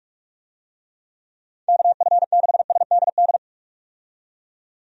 Morse code, heard as a keyed tone, about 46 wpm, 700 Hz. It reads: KF6SDD
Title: USA Call Signs - 6 Characters - 45wpm